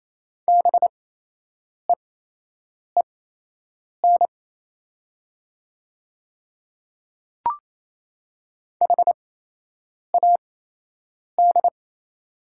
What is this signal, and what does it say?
Morse code, heard as a keyed tone, about 27 wpm, spaced as for 8 wpm, 700 Hz. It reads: BEEN HAD
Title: 8x Character Spacing - Top 100 Words - 28wpm